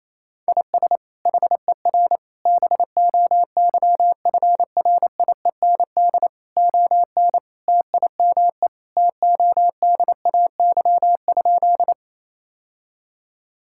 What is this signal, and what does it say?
Morse code, heard as a keyed tone, about 28 wpm, 700 Hz. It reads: IS HER BOYFRIEND ON TIME TODAY?
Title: Sentences from Top 1000 Words - Review Part 5 - 28wpm